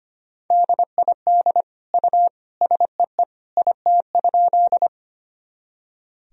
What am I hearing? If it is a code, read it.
Morse code, heard as a keyed tone, about 25 wpm, 700 Hz. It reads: DID U SEE IT?
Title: Sentences from Top 100 Words and Abbreviations 25wpm